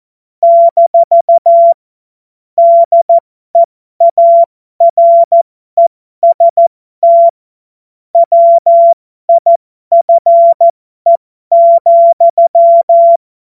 Morse code, heard as a keyed tone, about 14 wpm, spaced as for 12 wpm, 680 Hz. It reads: - DEAREST WIFE,
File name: Morsekurs / Witze